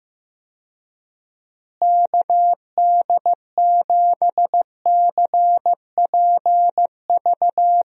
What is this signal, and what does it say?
Morse code, heard as a keyed tone, about 16 wpm, 700 Hz. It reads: KD7CPV
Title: USA Call Signs - 6 Characters - Repeated 2x - 15wpm